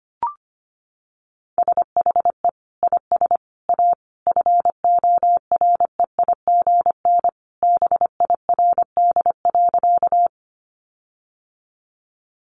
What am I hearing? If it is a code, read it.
Morse code, heard as a keyed tone, about 25 wpm, 700 Hz. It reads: SHE IS A FOREIGN BIRD.
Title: Sentences from Top 1400 Words 25wpm